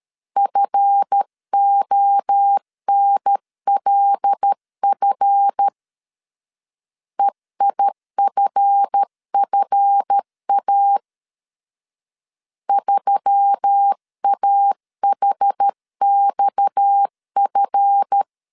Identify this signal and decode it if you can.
Morse code, heard as a keyed tone, about 13 wpm, 800 Hz. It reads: FONLF EIFFA 3AHXF
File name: LEZIONI / 7 GF9Z / 7 GF 12wpm 2122